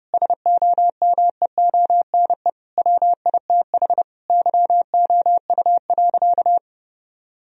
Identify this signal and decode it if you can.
Morse code, heard as a keyed tone, about 30 wpm, 700 Hz. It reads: SOMEONE WITH YOU.
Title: Sentences from Top 1000 Words - Review Part 4 - 30wpm